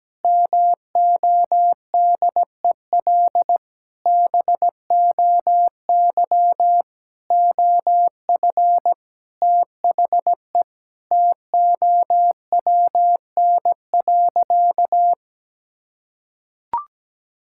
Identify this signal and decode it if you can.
Morse code, heard as a keyed tone, about 17 wpm, 700 Hz. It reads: MODEL BOY OF THE TOWN.
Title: Sentences from Top 1000 Words - Review Part 5 - 17wpm